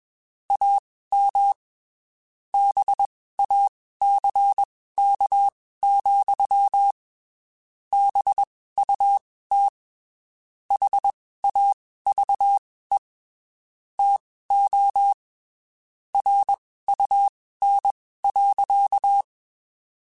Morse code, heard as a keyed tone, about 21 wpm, spaced as for 15 wpm, 800 Hz. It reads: AM BACK, BUT HAVE TO RUN.
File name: (spaced) / QSOs Spaced-mp3s / WA7QOX 2 @20 sp